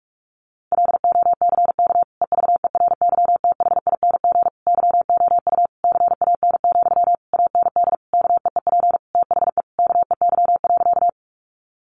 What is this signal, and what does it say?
Morse code, heard as a keyed tone, about 44 wpm, 700 Hz. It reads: LOCK EVERYTHING YOU CAN, AND KEEP THE KEY.
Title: Sentences from Top 1300 Words 45wpm